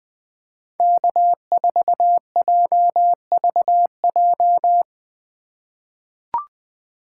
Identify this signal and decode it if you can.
Morse code, heard as a keyed tone, about 20 wpm, 700 Hz. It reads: K4JVJ
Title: USA Call Signs - 5 Characters - 4x Character Spacing - 20wpm